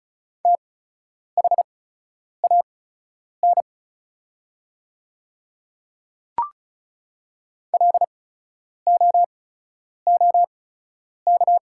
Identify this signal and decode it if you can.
Morse code, heard as a keyed tone, about 36 wpm, spaced as for 10 wpm, 700 Hz. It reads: THAN LOOK
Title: 8x Character Spacing - Top 100 Words - 35wpm